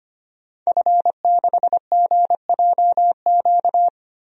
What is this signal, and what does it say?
Morse code, heard as a keyed tone, about 25 wpm, 700 Hz. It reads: F6GJQ
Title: DXCC-100 Call Signs - 5 Characters - Repeated 2x - 25wpm